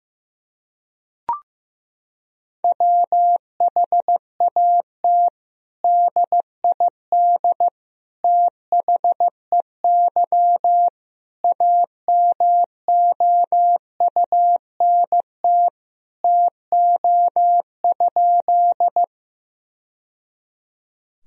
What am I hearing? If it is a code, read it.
Morse code, heard as a keyed tone, about 15 wpm, 700 Hz. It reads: WHAT DID THEY AMOUNT TO?
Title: Sentences from Top 1000 Words - Review Part 4 - 15wpm